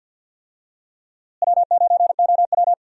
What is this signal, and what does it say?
Morse code, heard as a keyed tone, about 49 wpm, 700 Hz. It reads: W9OW
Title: USA Call Signs - 4 Characters - Repeated 2x - 50wpm